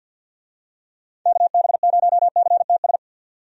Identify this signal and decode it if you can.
Morse code, heard as a keyed tone, about 47 wpm, 700 Hz. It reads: KB0CTS
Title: USA Call Signs - 6 Characters - 50wpm